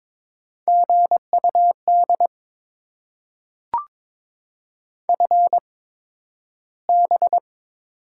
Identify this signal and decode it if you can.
Morse code, heard as a keyed tone, about 22 wpm, spaced as for 12 wpm, 700 Hz. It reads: GUD FB
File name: Top 20 QSO Elements - 8x Character Spacing - 22wpm